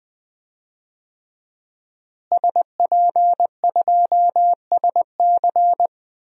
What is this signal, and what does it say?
Morse code, heard as a keyed tone, about 20 wpm, 700 Hz. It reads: SP2SC
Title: DXCC-100 Call Signs - 5 Characters - Repeated 3x - 20wpm